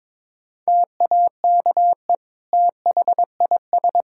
Morse code, heard as a keyed tone, about 22 wpm, 700 Hz. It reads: TAKE THIS
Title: Sentences from Top 1600 Words 22wpm